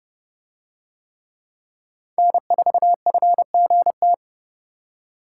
Morse code, heard as a keyed tone, about 27 wpm, 700 Hz. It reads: N4FGT